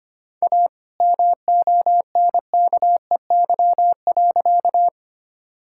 Morse code, heard as a keyed tone, about 25 wpm, 700 Hz. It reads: A MONKEY.
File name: practice / 25wpm / All-Age Appropriate Jokes - 25wpm